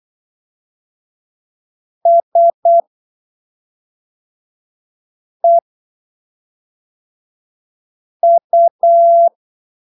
Morse code, heard as a keyed tone, about 8 wpm, spaced as for 3 wpm, 680 Hz. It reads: SEU
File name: arquivos / texto / texto 28 08ppm